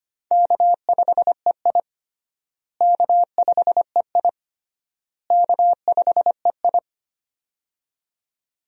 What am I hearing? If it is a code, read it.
Morse code, heard as a keyed tone, about 25 wpm, 700 Hz. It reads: K5EI K5EI K5EI
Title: USA Call Signs - 4 Characters - Repeated 3x - 25wpm